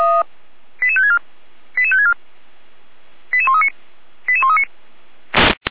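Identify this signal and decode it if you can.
Five-tone selective call (ZVEI), heard as quick runs of five tones, tones of about 70 ms. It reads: 88564 88564 88139 88139
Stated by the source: ZVEI five-tone selective call received by SDR.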